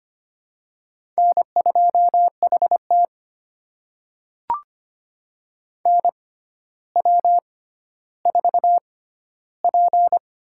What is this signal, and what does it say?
Morse code, heard as a keyed tone, about 25 wpm, spaced as for 13 wpm, 700 Hz. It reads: N2HT NW4P
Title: USA Call Signs - 4 Characters - 6x Character Spacing - 25wpm